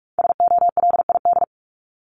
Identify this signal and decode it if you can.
Morse code, heard as a keyed tone, about 44 wpm, 700 Hz. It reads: SOLID
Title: Top 1251-1500 Words 45wpm